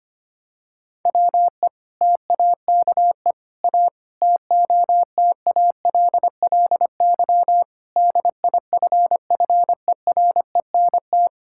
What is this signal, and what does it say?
Morse code, heard as a keyed tone, about 25 wpm, 700 Hz. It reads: WE TAKE A TOTALLY DIFFERENT
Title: Sentences from Top 1600 Words 25wpm